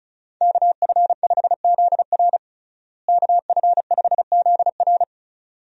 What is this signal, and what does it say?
Morse code, heard as a keyed tone, about 36 wpm, 700 Hz. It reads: KF5ZR KF5ZR
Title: USA Call Signs - 5 Characters - Repeated 2x - 35wpm